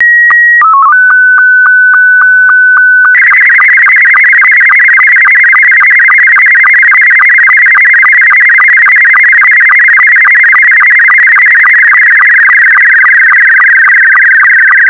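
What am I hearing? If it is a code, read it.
SSTV (slow-scan television), Scottie 2.
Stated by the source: SSTV signal in Scottie 2 mode, uploaded by KJ7RRV to PBARC.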